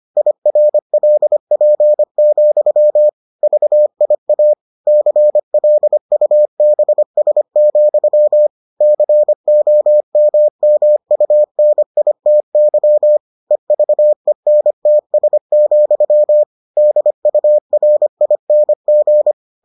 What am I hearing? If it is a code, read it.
Morse code, heard as a keyed tone, about 25 wpm, 600 Hz. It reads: IRLP, VIA CLUBS, COMMUNITY EVENTS, DURING